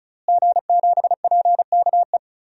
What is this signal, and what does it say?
Morse code, heard as a keyed tone, about 34 wpm, 700 Hz. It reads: G7PKE
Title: DXCC-100 Call Signs - 5 Characters - Repeated 3x - 35wpm